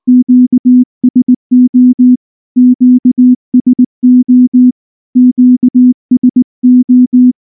Morse code, heard as a keyed tone, about 19 wpm, 250 Hz. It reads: QSO QSO QSO
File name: morse-training-20WPM